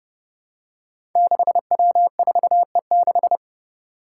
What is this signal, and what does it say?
Morse code, heard as a keyed tone, about 30 wpm, 700 Hz. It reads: 6W4E6
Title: Five Challenging Letter-Number Combinations 30wpm